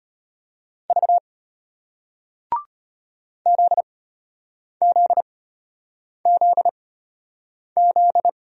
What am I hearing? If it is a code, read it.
Morse code, heard as a keyed tone, about 34 wpm, spaced as for 8 wpm, 700 Hz. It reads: V ZZZZ